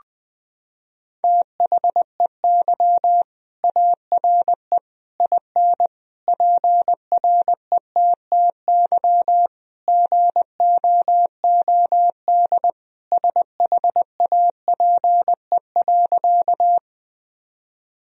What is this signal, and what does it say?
Morse code, heard as a keyed tone, about 20 wpm, 700 Hz. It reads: THEY ARE IN PRETTY GOOD SHAPE.
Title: Sentences from Top 1000 Words - Review Part 3 - 20wpm